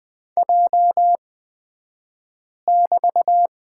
Morse code, heard as a keyed tone, about 20 wpm, spaced as for 5 wpm, 700 Hz. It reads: J<BT>
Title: 223 Lesson 36d - BT - Two Characters 20wpm